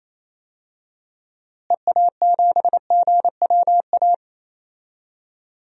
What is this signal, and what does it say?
Morse code, heard as a keyed tone, about 28 wpm, 700 Hz. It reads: EA7GWA
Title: DXCC-100 Call Signs - 6 Characters - Repeated 3x - 28wpm